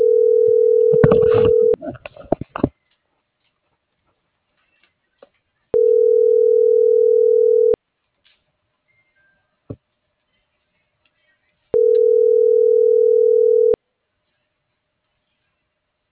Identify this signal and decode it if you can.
Ringback tone (North America), 440 + 480 Hz.